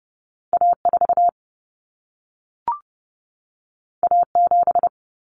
Morse code, heard as a keyed tone, about 30 wpm, 700 Hz. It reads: A4 A7
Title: Common Cut Numbers - 2-Digit 30wpm